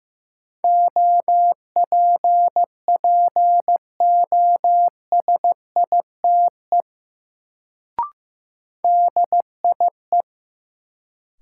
Morse code, heard as a keyed tone, about 15 wpm, 700 Hz. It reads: OPPOSITE DIE
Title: Sentences from Top 1300 Words - New Word Drill - 15wpm